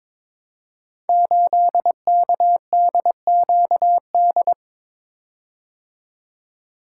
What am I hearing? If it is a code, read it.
Morse code, heard as a keyed tone, about 22 wpm, 700 Hz. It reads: 8KDQD
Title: Code Groups - 5 Letter:Numbers - 22wpm